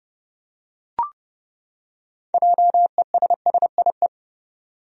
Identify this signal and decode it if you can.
Morse code, heard as a keyed tone, about 30 wpm, 700 Hz. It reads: JESSIE
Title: USA Common Names - 30wpm